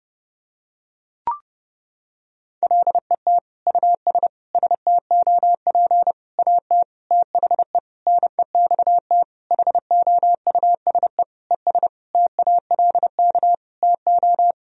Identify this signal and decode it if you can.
Morse code, heard as a keyed tone, about 30 wpm, 700 Hz. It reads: LET US STOP AT THE NEXT HOUSE ES TALK TO
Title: Sentences from Top 400 Words and Abbreviations 30wpm